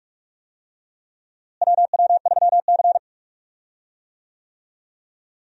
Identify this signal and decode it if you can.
Morse code, heard as a keyed tone, about 45 wpm, 700 Hz. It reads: WW3C
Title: USA Call Signs - 4 Characters - Repeated 3x - 45wpm